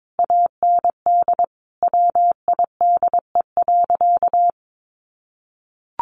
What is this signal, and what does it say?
Morse code, heard as a keyed tone, about 22 wpm, 700 Hz. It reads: AND WIDE.
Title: Sentences from Top 1200 Words 22wpm